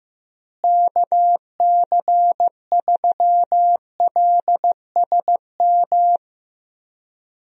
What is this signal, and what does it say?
Morse code, heard as a keyed tone, about 16 wpm, 700 Hz. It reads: KC3LSM
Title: USA Call Signs - 6 Characters - Repeated 2x - 15wpm